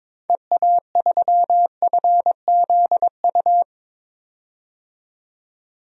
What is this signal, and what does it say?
Morse code, heard as a keyed tone, about 22 wpm, 700 Hz. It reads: EA3FZU
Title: DXCC-100 Call Signs - 6 Characters - 22wpm